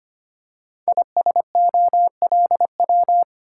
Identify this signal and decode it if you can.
Morse code, heard as a keyed tone, about 25 wpm, 700 Hz. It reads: ISOLW
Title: Code Groups - 5 Letter:Numbers - 25wpm